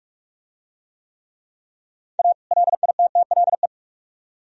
Morse code, heard as a keyed tone, about 45 wpm, 700 Hz. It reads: A LITTLE